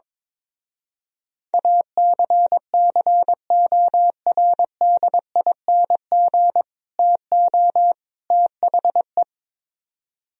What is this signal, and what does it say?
Morse code, heard as a keyed tone, about 22 wpm, 700 Hz. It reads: ACCORDING TO THE
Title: Speed-Racing - Sets of 3 Words 22wpm